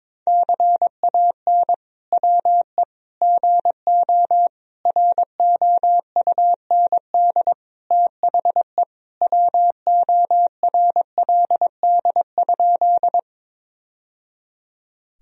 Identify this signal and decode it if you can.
Morse code, heard as a keyed tone, about 22 wpm, 700 Hz. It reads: CAN WE GO ROUND THE WORLD?
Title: Sentences from Top 1000 Words - Review Part 1 - 22wpm